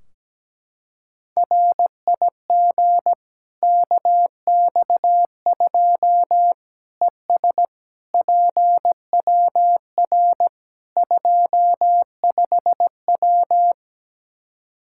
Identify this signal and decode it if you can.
Morse code, heard as a keyed tone, about 17 wpm, 700 Hz. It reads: RIG KX2 ES PWR 25W
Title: Practice QSO - Exchange 2 - Info - 17wpm